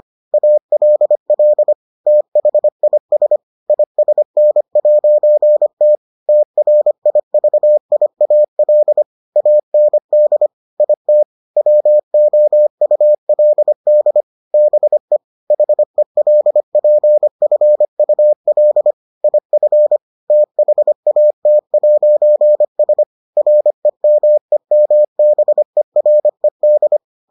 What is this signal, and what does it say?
Morse code, heard as a keyed tone, about 25 wpm, 600 Hz. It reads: ALL THIS ISN'T TRIVIAL AND IT WOULD BE HELPFUL IF THAT'S REMEMBERED